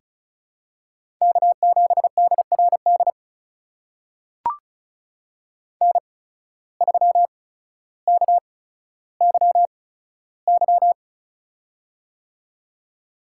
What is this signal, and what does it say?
Morse code, heard as a keyed tone, about 36 wpm, spaced as for 15 wpm, 700 Hz. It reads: K7DRD N3KYY
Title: USA Call Signs - 5 Characters - 8x Character Spacing - 35wpm